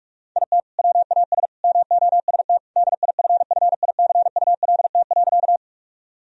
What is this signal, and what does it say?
Morse code, heard as a keyed tone, about 46 wpm, 700 Hz. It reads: IT WAS MOST DIFFICULT.
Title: Sentences from Top 1000 Words - Review Part 1 - 45wpm